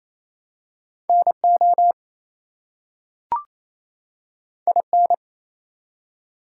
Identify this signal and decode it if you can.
Morse code, heard as a keyed tone, about 27 wpm, 700 Hz. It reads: NO IN
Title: Two Letter Words - 28wpm